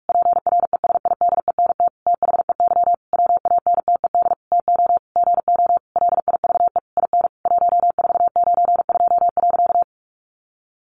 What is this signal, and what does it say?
Morse code, heard as a keyed tone, about 44 wpm, 700 Hz. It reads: PRESIDENT THEY WANTED TO GO LIVE IN 1492.
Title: A Sound Of Thunder 45wpm